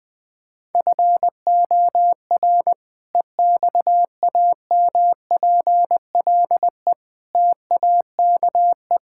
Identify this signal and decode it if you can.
Morse code, heard as a keyed tone, about 20 wpm, 700 Hz. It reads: FOR EXAMPLE TAKE
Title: Sentences with 2-Word Phrases - 20wpm